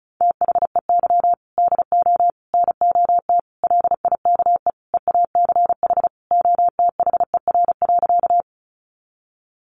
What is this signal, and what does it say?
Morse code, heard as a keyed tone, about 34 wpm, 700 Hz. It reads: THEY DO NOT LIKE EACH OTHER.
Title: Easy Phrases 35wpm